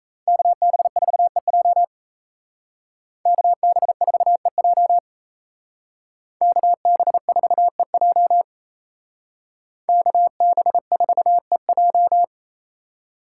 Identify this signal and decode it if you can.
Morse code, heard as a keyed tone, about 34 wpm, 700 Hz. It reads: KB4EJ KB4EJ KB4EJ KB4EJ
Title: Speed Racing - USA 2x2 Call Signs 28wpm